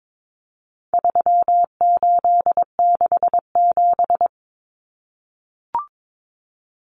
Morse code, heard as a keyed tone, about 22 wpm, 700 Hz. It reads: 3867
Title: Numbers - 4-Digit 22wpm